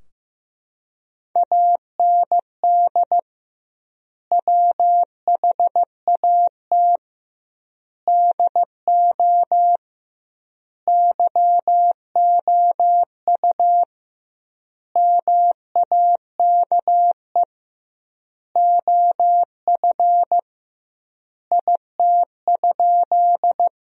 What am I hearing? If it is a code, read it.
Morse code, heard as a keyed tone, about 15 wpm, 700 Hz. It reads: AND WHAT DO YOU MAKE OF IT?
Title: 2x Word-Spacing - Sentences from Top 100 Words - 15wpm